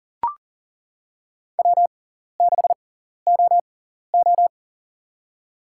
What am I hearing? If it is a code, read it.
Morse code, heard as a keyed tone, about 40 wpm, spaced as for 14 wpm, 700 Hz. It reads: W6OO
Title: USA Call Signs - 4 Characters - 6x Character Spacing - 40wpm